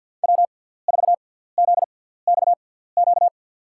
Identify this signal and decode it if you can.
Morse code, heard as a keyed tone, about 52 wpm, spaced as for 17 wpm, 700 Hz. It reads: W4ZXQ
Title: USA Call Signs - 6x Character Spacing - 50wpm